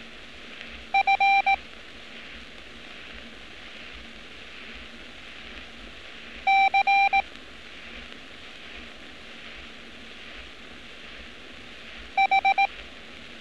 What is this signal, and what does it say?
Morse code, heard as a keyed tone, about 18 wpm, 780 Hz. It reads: F C H
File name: training / NAVPERS 11200-11211 / NAVPERS 11206 RA RB